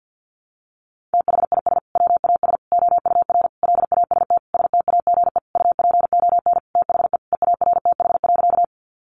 Morse code, heard as a keyed tone, about 50 wpm, 700 Hz. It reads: THIS WAS OUR LAST STAGE UPON THE EARTH.
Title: Sentences from Top 1200 Words 50wpm